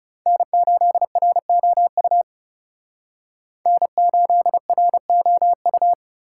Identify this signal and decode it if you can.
Morse code, heard as a keyed tone, about 39 wpm, 700 Hz. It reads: N8ROU N8ROU